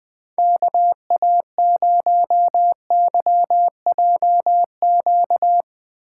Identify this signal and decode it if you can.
Morse code, heard as a keyed tone, about 21 wpm, 700 Hz. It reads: KA0YJQ
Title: Speed Racing - USA Call Signs 17wpm